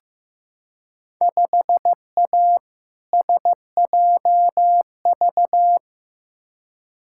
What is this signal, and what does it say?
Morse code, heard as a keyed tone, about 15 wpm, 700 Hz. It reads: 5A SJV
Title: Field Day Exchanges - 15wpm